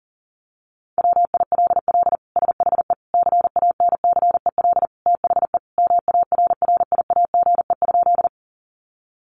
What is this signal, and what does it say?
Morse code, heard as a keyed tone, about 37 wpm, 700 Hz. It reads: WILL SHE CANCEL THE MARRIAGE?
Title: Sentences from Top 1200 Words 40wpm